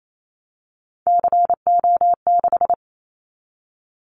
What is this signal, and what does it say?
Morse code, heard as a keyed tone, about 28 wpm, 700 Hz. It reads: CO6
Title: Callsign Prefixes for Top 100 DXCC - 28wpm